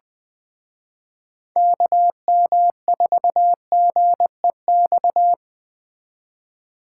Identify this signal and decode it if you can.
Morse code, heard as a keyed tone, about 20 wpm, 700 Hz. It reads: KM4GEX